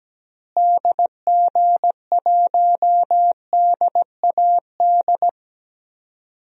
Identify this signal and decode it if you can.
Morse code, heard as a keyed tone, about 17 wpm, 700 Hz. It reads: DG1DAD
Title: DXCC-100 Call Signs - 6 Characters - Repeated 2x - 17wpm